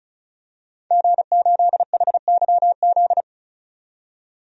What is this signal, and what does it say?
Morse code, heard as a keyed tone, about 34 wpm, 700 Hz. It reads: G8HYZ
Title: DXCC-100 Call Signs - 5 Characters - 35wpm